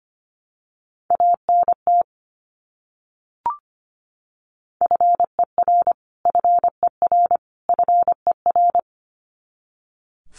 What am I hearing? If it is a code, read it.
Morse code, heard as a keyed tone, about 25 wpm, 700 Hz. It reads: ANT FER FER FER
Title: Abbreviations Repeated 3x - 25wpm